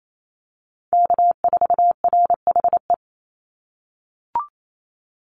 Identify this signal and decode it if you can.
Morse code, heard as a keyed tone, about 28 wpm, 700 Hz. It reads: K4RHE
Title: USA Call Signs 28wpm